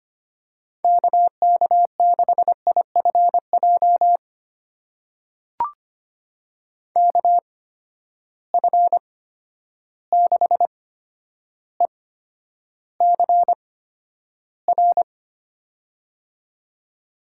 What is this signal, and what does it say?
Morse code, heard as a keyed tone, about 25 wpm, spaced as for 11 wpm, 700 Hz. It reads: KK6IFJ KF6ECR
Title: USA Call Signs - 6 Characters - 8x Character Spacing - 25wpm